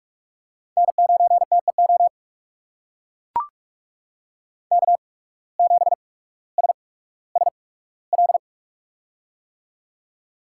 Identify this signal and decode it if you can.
Morse code, heard as a keyed tone, about 44 wpm, spaced as for 19 wpm, 700 Hz. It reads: N9TEO K7SSL
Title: USA Call Signs - 5 Characters - 8x Character Spacing - 45wpm